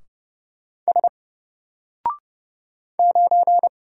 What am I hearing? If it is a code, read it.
Morse code, heard as a keyed tone, about 30 wpm, 700 Hz. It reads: S 9
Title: Single Character (Letter:Number:Punctuation:Prosign) Alt 30wpm